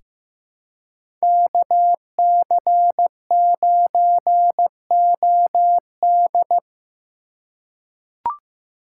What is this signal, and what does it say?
Morse code, heard as a keyed tone, about 15 wpm, 700 Hz. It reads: KC9OD
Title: USA Call Signs - 5 Characters - 2x Character Spacing - 15wpm